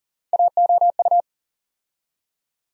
Morse code, heard as a keyed tone, about 40 wpm, 700 Hz. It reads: AOU
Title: USA Call Sign Postfixes - 3 Letters 40wpm